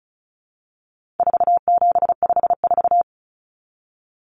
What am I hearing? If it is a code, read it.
Morse code, heard as a keyed tone, about 34 wpm, 700 Hz. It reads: 4754
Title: Common Cut Numbers - 4-Digit 35wpm